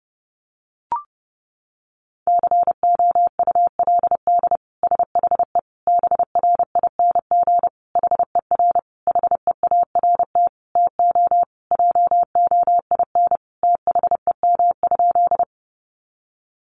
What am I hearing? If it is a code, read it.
Morse code, heard as a keyed tone, about 30 wpm, 700 Hz. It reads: COULD SHE BRING HER HEART TO JOIN THEM?
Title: Sentences from Top 1400 Words 30wpm